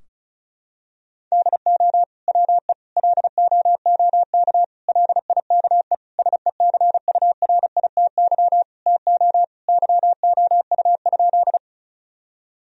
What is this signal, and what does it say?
Morse code, heard as a keyed tone, about 36 wpm, 700 Hz. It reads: DO WE LOOK LIKE SECURITY TO YOU?
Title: Sentences from Top 1000 Words - Review Part 4 - 35wpm